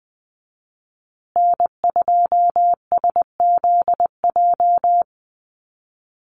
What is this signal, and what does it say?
Morse code, heard as a keyed tone, about 20 wpm, 700 Hz. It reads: N2SZJ
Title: USA Call Signs Repeated 2x 20wpm